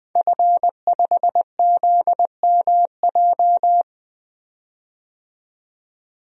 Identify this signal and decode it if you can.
Morse code, heard as a keyed tone, about 20 wpm, 700 Hz. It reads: F5ZMJ
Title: DXCC-100 Call Signs - 5 Characters - 20wpm